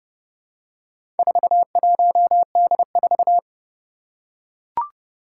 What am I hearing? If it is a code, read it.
Morse code, heard as a keyed tone, about 30 wpm, 700 Hz. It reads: 41D4
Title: Four Challenging Letter-Number Combinations 30wpm